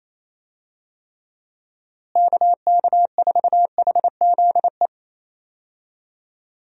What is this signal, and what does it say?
Morse code, heard as a keyed tone, about 28 wpm, 700 Hz. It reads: KK4HZE
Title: USA Call Signs - 6 Characters - Repeated 2x - 28wpm